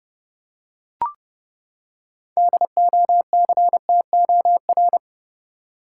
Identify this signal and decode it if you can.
Morse code, heard as a keyed tone, about 30 wpm, 700 Hz. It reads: DOCTOR
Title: Six Letter Words - 30wpm